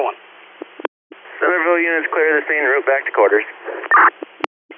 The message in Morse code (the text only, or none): none